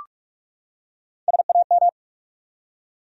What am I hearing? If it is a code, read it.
Morse code, heard as a keyed tone, about 45 wpm, 700 Hz. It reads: SAM
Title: USA Common Names - 45wpm